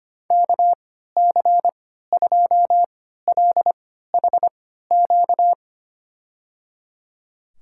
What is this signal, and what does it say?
Morse code, heard as a keyed tone, about 25 wpm, spaced as for 14 wpm, 700 Hz. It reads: KC2LHQ
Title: USA Call Signs - 6 Characters - 3x Character Spacing - 25wpm